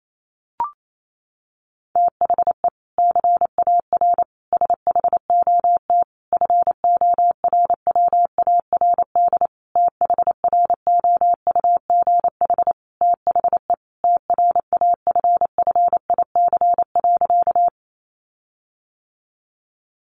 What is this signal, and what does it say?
Morse code, heard as a keyed tone, about 28 wpm, 700 Hz. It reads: THE CAR SHOT FORWARD THROUGH THE TRAFFIC.
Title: Sentences from Top 1300 Words 28wpm